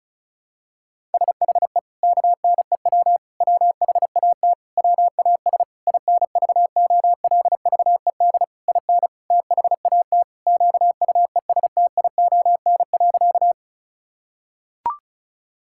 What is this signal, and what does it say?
Morse code, heard as a keyed tone, about 36 wpm, 700 Hz. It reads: SHE KNEW WHAT WAS INVOLVED IN THAT QUESTION.
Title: Sentences from Top 1000 Words - Review Part 5 - 35wpm